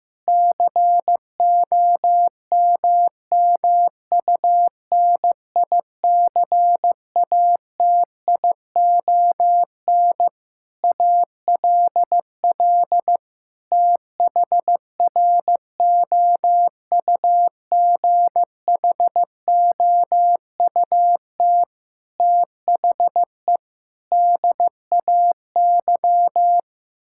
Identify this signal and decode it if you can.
Morse code, heard as a keyed tone, about 15 wpm, 700 Hz. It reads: COMMUNICATION ALL THROUGHOUT THE DAY
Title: Sentences from Top 1000 Words - Review Part 6 - 15wpm